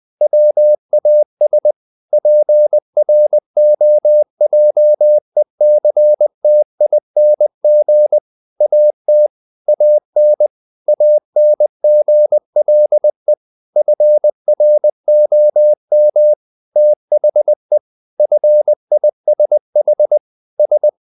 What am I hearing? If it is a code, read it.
Morse code, heard as a keyed tone, about 20 wpm, 600 Hz. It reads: WAS PROJECTING AT AN ANGLE FROM THE FISH S
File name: the old man and the sea part 2 20wpm